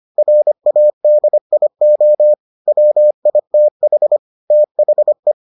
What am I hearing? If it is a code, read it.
Morse code, heard as a keyed tone, about 25 wpm, 600 Hz. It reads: RADIO WITH THE